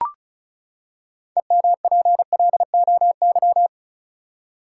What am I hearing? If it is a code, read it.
Morse code, heard as a keyed tone, about 34 wpm, 700 Hz. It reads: EMPLOY
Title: Six Letter Words - 35wpm